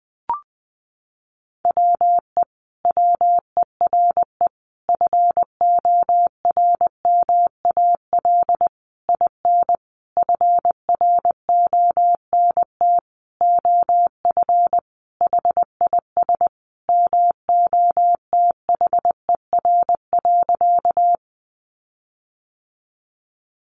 Morse code, heard as a keyed tone, about 19 wpm, 700 Hz. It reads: WE WERE FORMAL IN FRONT OF HIS MOTHER.
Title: Sentences from Top 1300 Words 20wpm